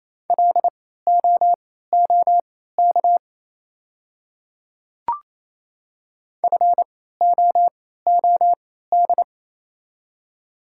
Morse code, heard as a keyed tone, about 28 wpm, spaced as for 16 wpm, 700 Hz. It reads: LOOK FOOD
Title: 3x Character Spacing - Top 300 Words - 28wpm